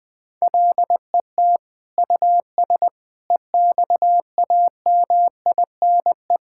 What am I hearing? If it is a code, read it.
Morse code, heard as a keyed tone, about 20 wpm, 700 Hz. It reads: LET US EXAMINE
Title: Sentences from Top 1600 Words 20wpm